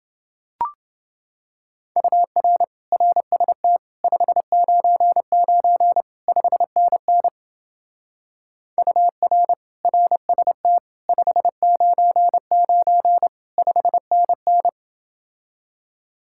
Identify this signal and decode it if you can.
Morse code, heard as a keyed tone, about 29 wpm, 700 Hz. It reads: UR RST 599 5NN UR RST 599 5NN